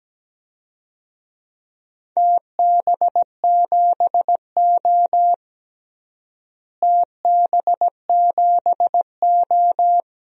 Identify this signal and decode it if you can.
Morse code, heard as a keyed tone, about 17 wpm, 700 Hz. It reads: TB7O TB7O
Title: DXCC-100 Call Signs - 4 Characters - Repeated 2x - 17wpm